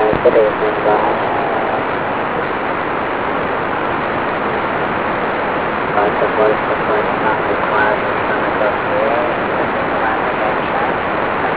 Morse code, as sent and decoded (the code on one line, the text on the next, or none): none